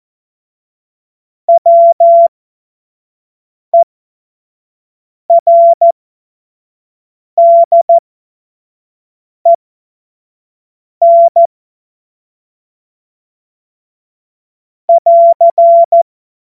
Morse code, heard as a keyed tone, about 14 wpm, spaced as for 5 wpm, 680 Hz. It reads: WERDEN <AR>
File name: Morsekurs / Witze / Witz 11. April 2023 25 F70